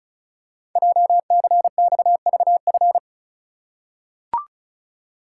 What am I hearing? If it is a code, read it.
Morse code, heard as a keyed tone, about 36 wpm, 700 Hz. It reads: JCXVF
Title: Code Groups - 5 Letters - 35wpm